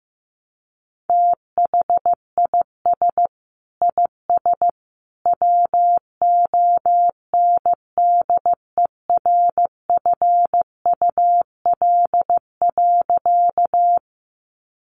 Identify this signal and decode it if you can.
Morse code, heard as a keyed tone, about 15 wpm, 700 Hz. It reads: THIS IS WONDERFUL.
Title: Sentences from Top 1100 Words 15wpm